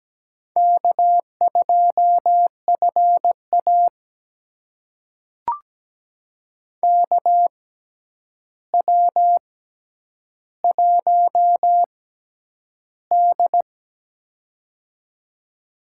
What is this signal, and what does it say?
Morse code, heard as a keyed tone, about 17 wpm, spaced as for 9 wpm, 700 Hz. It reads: K2FA KW1D
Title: USA Call Signs - 6x Character Spacing - 17wpm